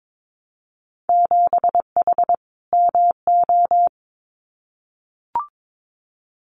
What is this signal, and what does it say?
Morse code, heard as a keyed tone, about 22 wpm, 700 Hz. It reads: 7H MO
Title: Winter Field Day Exchanges Encore 22wpm